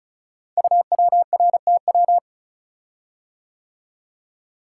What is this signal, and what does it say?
Morse code, heard as a keyed tone, about 36 wpm, 690 Hz. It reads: UWRTW
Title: Code Groups - 5 Letters - 35wpm